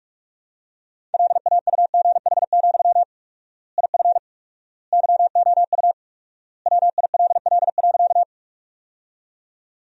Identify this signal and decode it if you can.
Morse code, heard as a keyed tone, about 41 wpm, 700 Hz. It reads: LAUGH, IF YOU WILL.
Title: 4x Word-Spacing - Sentences from Top 500 Words - 45wpm